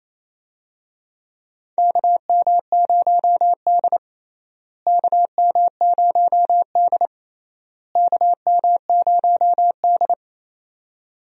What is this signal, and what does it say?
Morse code, heard as a keyed tone, about 28 wpm, 700 Hz. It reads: KM0D KM0D KM0D
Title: USA Call Signs - 4 Characters - Repeated 3x - 28wpm